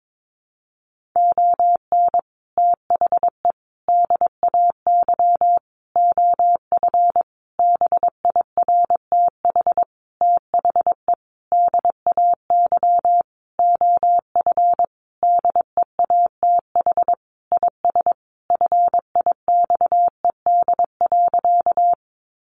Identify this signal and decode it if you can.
Morse code, heard as a keyed tone, about 22 wpm, 700 Hz. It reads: ON THE DAY OF BIRTH THE DAY OF DEATH IS FIXED.
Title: Sentences from Top 1100 Words 22wpm